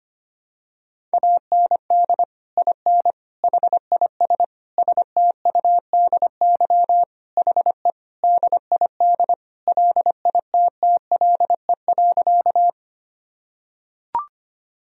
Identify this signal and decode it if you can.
Morse code, heard as a keyed tone, about 25 wpm, 700 Hz. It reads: AND IN HIS STUDY HE DID LITTLE.
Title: Sentences from Top 1000 Words - Review Part 1 - 25wpm